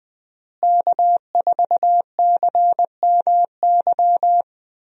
Morse code, heard as a keyed tone, about 20 wpm, 700 Hz. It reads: K4CMY
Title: USA Call Signs - 5 Characters - Repeated 2x - 20wpm